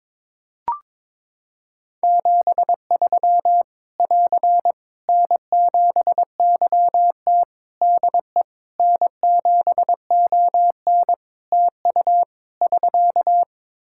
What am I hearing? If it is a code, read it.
Morse code, heard as a keyed tone, about 21 wpm, 700 Hz. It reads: 73 <AR> N7YT DE N7ON TU <SK>
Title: Practice QSO - Exchange 4 - Ending - 22wpm